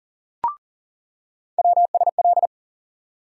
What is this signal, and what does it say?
Morse code, heard as a keyed tone, about 40 wpm, 700 Hz. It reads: WSL
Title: USA Call Sign Postfixes - 3 Letters 40wpm